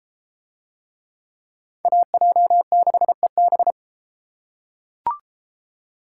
Morse code, heard as a keyed tone, about 32 wpm, 700 Hz. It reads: AJ6EB